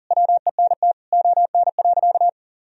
Morse code, heard as a keyed tone, about 40 wpm, 700 Hz. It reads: WENT ON.
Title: Sentences from Top 1000 Words - Review Part 3 - 40wpm